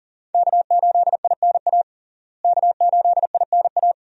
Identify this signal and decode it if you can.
Morse code, heard as a keyed tone, about 37 wpm, 700 Hz. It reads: K8INA K8INA
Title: USA Call Signs - 5 Characters - Repeated 2x - 40wpm